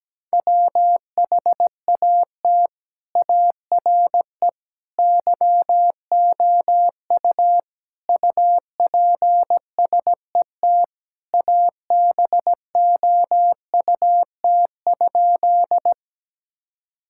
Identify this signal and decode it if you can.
Morse code, heard as a keyed tone, about 17 wpm, 700 Hz. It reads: WHAT ARE YOU UPSET ABOUT?
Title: Sentences from Top 1000 Words - Review Part 5 - 17wpm